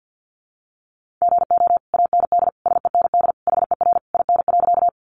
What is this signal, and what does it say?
Morse code, heard as a keyed tone, about 50 wpm, 700 Hz. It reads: GO AND SEND HER IN.
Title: Sentences from Top 1500 Words 50wpm